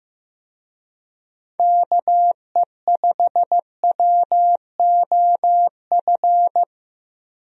Morse code, heard as a keyed tone, about 15 wpm, 700 Hz. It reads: KE5WOF
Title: USA Call Signs - 6 Characters - Repeated 3x - 15wpm